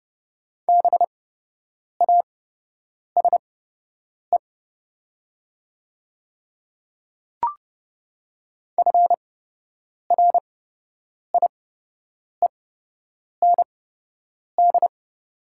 Morse code, heard as a keyed tone, about 30 wpm, spaced as for 8 wpm, 700 Hz. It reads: BASE FRIEND
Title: 8x Character Spacing - Top 300 Words - 30wpm